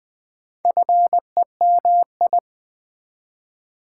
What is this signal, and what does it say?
Morse code, heard as a keyed tone, about 20 wpm, 700 Hz. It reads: FEMI
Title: Build A Word - 2 Letters At A Time - 8-Letter Words - 20wpm